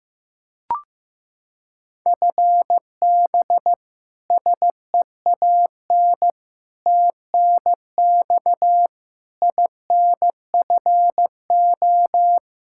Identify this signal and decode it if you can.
Morse code, heard as a keyed tone, about 15 wpm, 700 Hz. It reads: FB SEAN TNX INFO
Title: Practice QSO - Exchange 3 - More Info - 15wpm